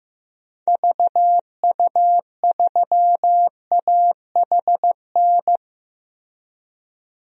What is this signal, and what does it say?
Morse code, heard as a keyed tone, about 15 wpm, 700 Hz. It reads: VU3AHN
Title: DXCC-100 Call Signs - 6 Characters - 15wpm